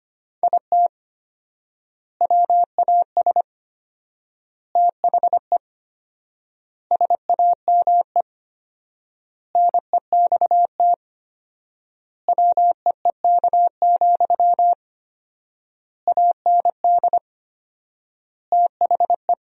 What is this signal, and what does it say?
Morse code, heard as a keyed tone, about 25 wpm, 700 Hz. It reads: IT WAS THE SAME NEXT WEEK, AND THE
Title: 4x Word-Spacing - Sentences from Top 500 Words - 25wpm